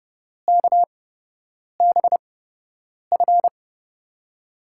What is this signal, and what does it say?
Morse code, heard as a keyed tone, about 30 wpm, spaced as for 8 wpm, 700 Hz. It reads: KBF
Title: USA Call Signs - 6 Characters - 8x Character Spacing - 30wpm